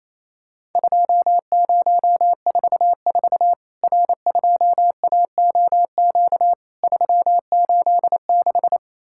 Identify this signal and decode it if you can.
Morse code, heard as a keyed tone, about 28 wpm, 700 Hz. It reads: 2044 R2AOQ 386
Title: Worked All Europe DX Contest - QTC Exchange - 28wpm